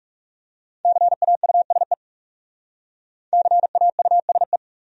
Morse code, heard as a keyed tone, about 38 wpm, 700 Hz. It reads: CAUSE CAUSE